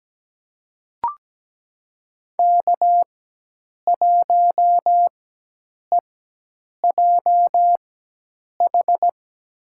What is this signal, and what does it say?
Morse code, heard as a keyed tone, about 17 wpm, spaced as for 8 wpm, 700 Hz. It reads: K1EJH
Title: USA Call Signs - 5 Characters - 4x Character Spacing - 17wpm